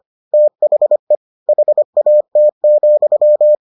Morse code, heard as a keyed tone, about 25 wpm, 600 Hz. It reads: THE HAT,